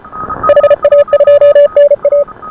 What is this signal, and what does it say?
Morse code, heard as a keyed tone, about 29 wpm, 590 Hz. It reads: HA2NA